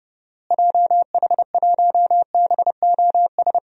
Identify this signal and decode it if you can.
Morse code, heard as a keyed tone, about 29 wpm, 700 Hz. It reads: JH1BOS
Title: DXCC-100 Call Signs - 6 Characters - Repeated 2x - 30wpm